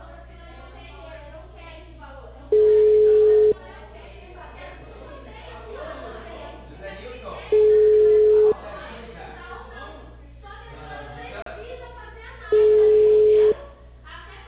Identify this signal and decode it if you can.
Ringback tone (Europe, 425 Hz), with speech.
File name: out- -2012- - - .78633